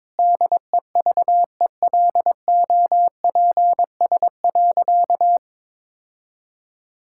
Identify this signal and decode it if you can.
Morse code, heard as a keyed tone, about 22 wpm, 700 Hz. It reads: DEVELOPS.
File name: Sentences with 7-Word Phrases - 22wpm